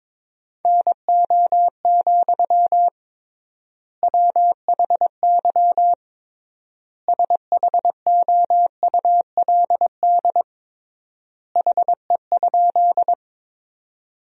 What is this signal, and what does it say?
Morse code, heard as a keyed tone, about 22 wpm, 700 Hz. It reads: NO, WHY SHOULD HE?
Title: 3x Word-Spacing - Sentences from Top 300 Words - 22wpm